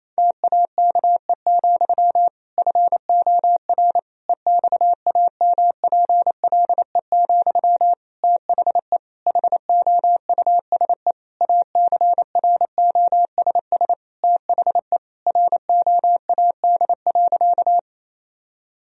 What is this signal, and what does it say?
Morse code, heard as a keyed tone, about 28 wpm, 700 Hz. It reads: TAKE, FOR EXAMPLE, THE HOUSE ACROSS THE ROAD.
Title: Sentences from Top 1000 Words - Review Part 1 - 28wpm